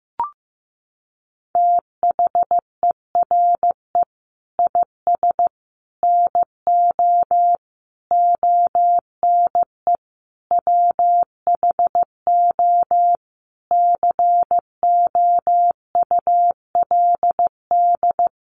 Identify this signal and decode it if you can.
Morse code, heard as a keyed tone, about 15 wpm, 700 Hz. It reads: THERE IS NO ONE WHO COULD
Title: Sentences from Top 1100 Words 15wpm